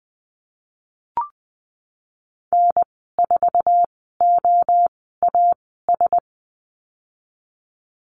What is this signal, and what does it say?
Morse code, heard as a keyed tone, about 19 wpm, spaced as for 14 wpm, 700 Hz. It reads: N4OAS
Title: 069 Lesson 12g - Letter d - Call Signs 20wpm